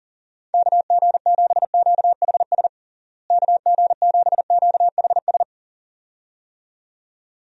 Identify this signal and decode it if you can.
Morse code, heard as a keyed tone, about 40 wpm, 700 Hz. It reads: KG7QHS KG7QHS
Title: USA Call Signs - 6 Characters - Repeated 2x - 40wpm